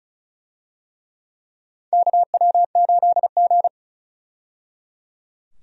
Morse code, heard as a keyed tone, about 36 wpm, 700 Hz. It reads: KW8G
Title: USA Call Signs - 4 Characters - 35wpm